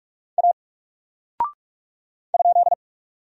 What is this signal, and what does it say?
Morse code, heard as a keyed tone, about 46 wpm, 700 Hz. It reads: A ?
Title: Single Character (Letter:Number:Punctuation:Prosign) 45wpm